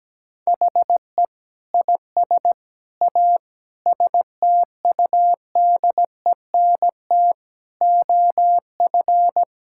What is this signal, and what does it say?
Morse code, heard as a keyed tone, about 17 wpm, 700 Hz. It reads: HE IS A STUDENT OF
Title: Sentences from Top 1000 Words - Review Part 6 - 17wpm